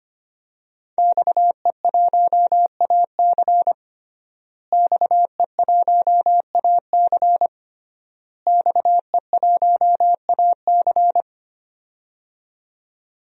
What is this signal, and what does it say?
Morse code, heard as a keyed tone, about 25 wpm, 700 Hz. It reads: XE1AC XE1AC XE1AC
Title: DXCC-100 Call Signs - 5 Characters - Repeated 3x - 25wpm